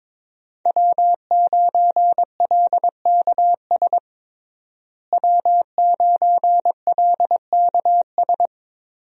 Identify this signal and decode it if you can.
Morse code, heard as a keyed tone, about 21 wpm, 700 Hz. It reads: W9LKS W9LKS
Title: USA Call Signs - 5 Characters - Repeated 2x - 22wpm